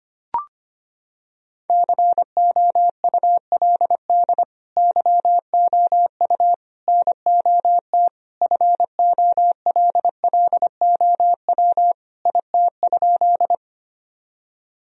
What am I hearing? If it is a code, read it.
Morse code, heard as a keyed tone, about 25 wpm, 700 Hz. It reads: COULD YOU NOT FOLLOW IT?
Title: Sentences from Top 1000 Words - Review Part 1 - 25wpm